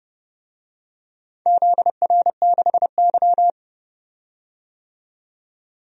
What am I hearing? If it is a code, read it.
Morse code, heard as a keyed tone, about 30 wpm, 700 Hz. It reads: ZR6Y
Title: DXCC-100 Call Signs - 4 Characters - 30wpm